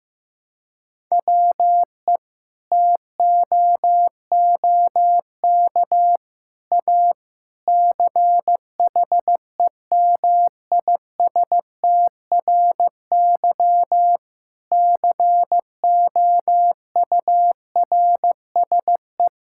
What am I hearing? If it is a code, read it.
Morse code, heard as a keyed tone, about 15 wpm, 700 Hz. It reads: WE TOOK A CHEMISTRY COURSE
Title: Sentences from Top 1000 Words - Review Part 6 - 15wpm